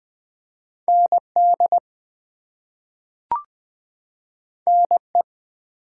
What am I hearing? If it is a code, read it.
Morse code, heard as a keyed tone, about 20 wpm, 700 Hz. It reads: ND NE
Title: US States and Canadian Provinces 20wpm